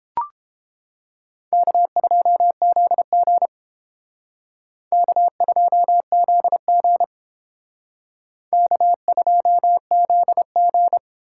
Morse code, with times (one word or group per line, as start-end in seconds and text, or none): 1.5-3.5 K2ZG
4.9-7.0 K2ZG
8.5-11.0 K2ZG